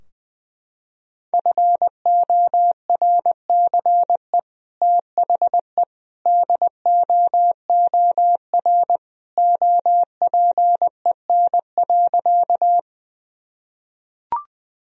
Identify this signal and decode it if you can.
Morse code, heard as a keyed tone, about 20 wpm, 700 Hz. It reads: FORCE THE DOOR OPEN.
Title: Sentences from Top 1000 Words - Review Part 3 - 20wpm